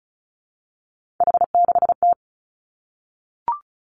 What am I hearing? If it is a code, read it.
Morse code, heard as a keyed tone, about 36 wpm, 700 Hz. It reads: H6T
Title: Three Challenging Letter-Number Combinations 35wpm